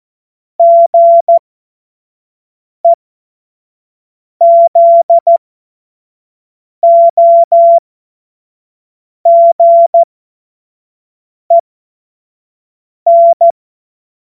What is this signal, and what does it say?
Morse code, heard as a keyed tone, about 14 wpm, spaced as for 5 wpm, 680 Hz. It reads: GEZOGEN
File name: Morsekurs / Witze / Witz 08. Februar 2022 25 F70